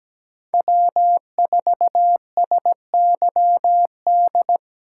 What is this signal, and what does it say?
Morse code, heard as a keyed tone, about 20 wpm, 700 Hz. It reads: W4SYD
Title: Speed Racing - USA 1x3 Call Signs 17wpm